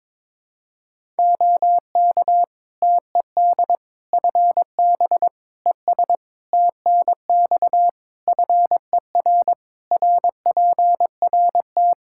Morse code, heard as a keyed tone, about 22 wpm, 700 Hz. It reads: OK TED FB ES TNX FER RPRT
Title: Practice QSO - Exchange 2 - Info - 22wpm